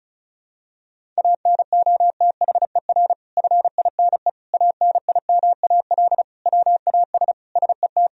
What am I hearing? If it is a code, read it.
Morse code, heard as a keyed tone, about 36 wpm, 700 Hz. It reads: ANOTHER FINE ANIMAL WAS SET
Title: Sentences from Top 1000 Words - Review Part 1 - 35wpm